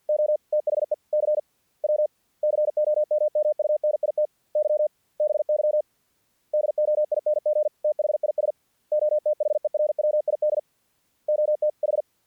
Morse code, heard as a keyed tone, about 48 wpm, 600 Hz. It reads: O THE CW COMMUNITY BY DOING THIS OTHERWID OT H
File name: archived / CRSnet